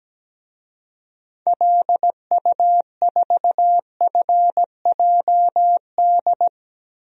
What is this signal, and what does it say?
Morse code, heard as a keyed tone, about 17 wpm, 700 Hz. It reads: LU4FJD
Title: DXCC-100 Call Signs - 6 Characters - Repeated 3x - 17wpm